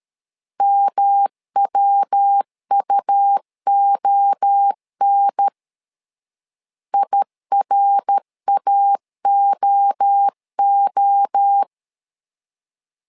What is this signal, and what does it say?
Morse code, heard as a keyed tone, about 13 wpm, 800 Hz. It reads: MWUON IRAOO
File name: LEZIONI / 3 MWRO / 3 MWRO 12wpm 1730